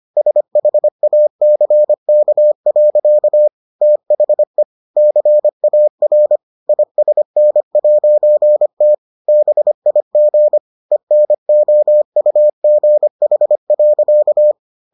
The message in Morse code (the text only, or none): SHACK. THE CAR ISN'T BIG ENOUGH.